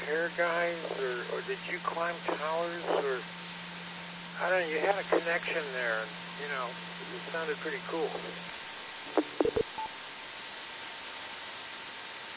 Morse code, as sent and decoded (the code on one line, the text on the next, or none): none